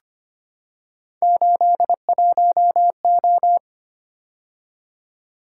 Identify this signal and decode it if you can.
Morse code, heard as a keyed tone, about 25 wpm, 700 Hz. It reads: 81O
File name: Three Letter-Number Combinations - 25wpm